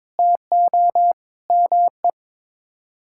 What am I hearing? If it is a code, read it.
Morse code, heard as a keyed tone, about 22 wpm, 700 Hz. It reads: TO ME
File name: Sets of 9 Words - 22wpm